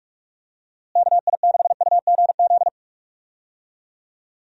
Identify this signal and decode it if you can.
Morse code, heard as a keyed tone, about 42 wpm, 700 Hz. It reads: KI6UGZ
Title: USA Call Signs - 6 Characters - Repeated 2x - 45wpm